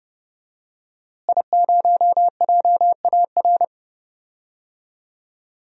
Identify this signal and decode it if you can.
Morse code, heard as a keyed tone, about 30 wpm, 700 Hz. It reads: I0JAR
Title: DXCC-100 Call Signs - 5 Characters - Repeated 3x - 30wpm